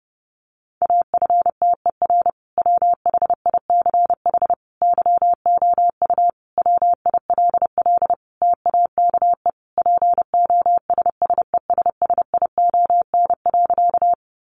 Morse code, heard as a keyed tone, about 30 wpm, 700 Hz. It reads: AFTER WHICH YOU WILL TAKE POSSESSION.
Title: Sentences from Top 1400 Words 30wpm